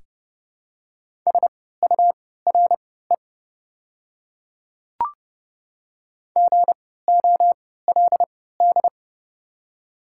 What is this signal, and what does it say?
Morse code, heard as a keyed tone, about 29 wpm, spaced as for 17 wpm, 700 Hz. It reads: SURE GOLD
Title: 3x Character Spacing - Top 500 Words - 30wpm